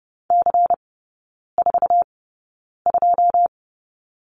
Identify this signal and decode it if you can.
Morse code, heard as a keyed tone, about 30 wpm, spaced as for 9 wpm, 700 Hz. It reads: C42
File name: Lesson 18e - Letter m - Three Characters 30wpm